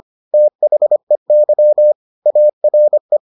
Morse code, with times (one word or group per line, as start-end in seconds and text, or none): 0.3-1.9 THEY
2.3-3.2 ARE